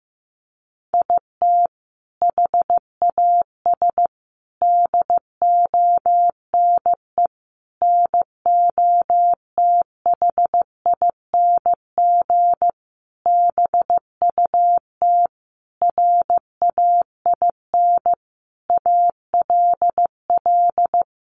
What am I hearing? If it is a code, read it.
Morse code, heard as a keyed tone, about 15 wpm, 700 Hz. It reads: IT HAS DONE NOTHING BUT RAIN ALL
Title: Sentences from Top 1100 Words 15wpm